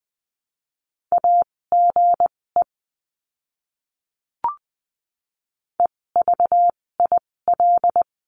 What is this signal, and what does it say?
Morse code, heard as a keyed tone, about 20 wpm, spaced as for 16 wpm, 700 Hz. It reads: AGE EVIL
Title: 256 Lesson 42a - Words 12-20wpm